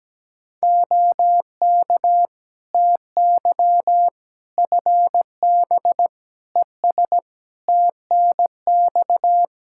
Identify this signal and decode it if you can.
Morse code, heard as a keyed tone, about 17 wpm, 700 Hz. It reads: OK TY FB ES TNX
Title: K1USN SST QSO Practice 17wpm